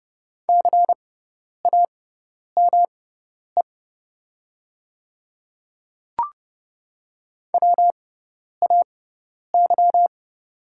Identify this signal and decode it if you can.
Morse code, heard as a keyed tone, about 30 wpm, spaced as for 10 wpm, 690 Hz. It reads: CAME WAY
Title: 6x Character Spacing - Top 200 Words - 30wpm